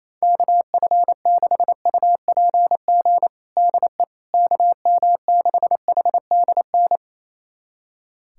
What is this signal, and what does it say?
Morse code, heard as a keyed tone, about 28 wpm, 700 Hz. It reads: KF6UPG DE KM6HDN
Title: USA Callsign DE Callsign - 6 Characters - 28wpm